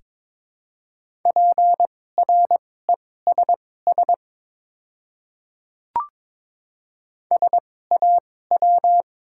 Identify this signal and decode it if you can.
Morse code, heard as a keyed tone, about 22 wpm, spaced as for 16 wpm, 700 Hz. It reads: PRESS SAW
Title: 2x Character Spacing - Top 500 Words - 22wpm